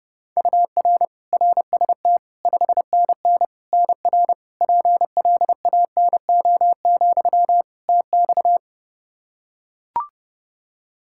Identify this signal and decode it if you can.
Morse code, heard as a keyed tone, about 30 wpm, 700 Hz. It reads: UR RST 5NN NR PLANO, TX
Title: Practice QSO - Exchange 1 - Beginning - 30wpm